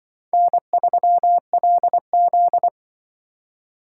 Morse code, heard as a keyed tone, about 24 wpm, 700 Hz. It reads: N3LZ
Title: Speed Racing - USA Call Signs 20wpm